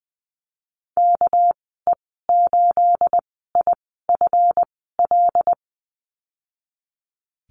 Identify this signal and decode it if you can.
Morse code, heard as a keyed tone, about 20 wpm, 700 Hz. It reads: K E 8 I F L
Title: 231 Lesson 37g - Letter x - Call Signs 20wpm